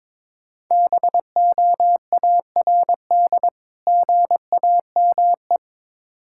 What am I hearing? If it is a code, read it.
Morse code, heard as a keyed tone, about 22 wpm, 700 Hz. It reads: BOARD GAME
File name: Difficult Words with Hints - 22wpm